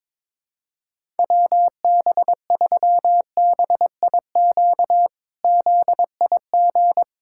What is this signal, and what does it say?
Morse code, heard as a keyed tone, about 21 wpm, 700 Hz. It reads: WB3BIQ ZIG
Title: K1USN SST QSO Practice - 22wpm